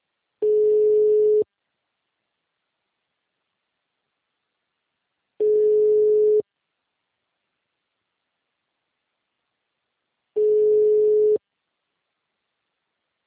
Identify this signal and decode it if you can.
Ringback tone (Europe, 425 Hz).
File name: out- -2023